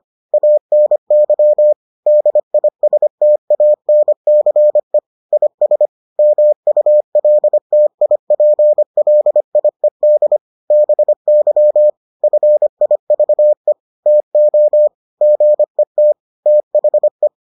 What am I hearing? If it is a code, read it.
Morse code, heard as a keyed tone, about 25 wpm, 600 Hz. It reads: ANY DISTANCE IS MULTIPLIED BY FIVE TO GET THE